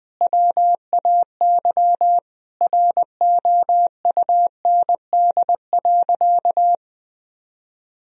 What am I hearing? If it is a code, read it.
Morse code, heard as a keyed tone, about 20 wpm, 700 Hz. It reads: WAY ROUND.
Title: Sentences from Top 1000 Words - Review Part 3 - 20wpm